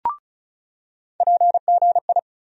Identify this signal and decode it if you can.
Morse code, heard as a keyed tone, about 34 wpm, 700 Hz. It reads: PGI
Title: USA Call Sign Postfixes - 3 Letters 35wpm